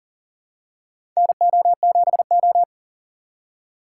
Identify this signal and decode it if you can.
Morse code, heard as a keyed tone, about 40 wpm, 700 Hz. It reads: NO7O
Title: USA Call Signs - 4x Character Spacing - 40wpm